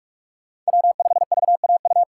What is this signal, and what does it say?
Morse code, heard as a keyed tone, about 45 wpm, 700 Hz. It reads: W5VAU
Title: USA Call Signs - 6x Character Spacing - 45wpm